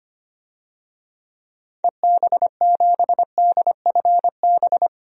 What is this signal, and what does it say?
Morse code, heard as a keyed tone, about 25 wpm, 700 Hz. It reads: EB7DFB
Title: DXCC-100 Call Signs - 6 Characters - 25wpm